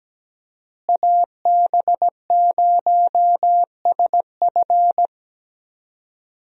Morse code, heard as a keyed tone, about 17 wpm, 700 Hz. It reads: AB0SF
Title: USA Call Signs - 5 Characters - Repeated 3x - 17wpm